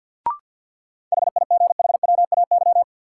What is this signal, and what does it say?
Morse code, heard as a keyed tone, about 47 wpm, 700 Hz. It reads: HIGHWAY
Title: Sentences from Top 1400 Words - New Word Drill - 50wpm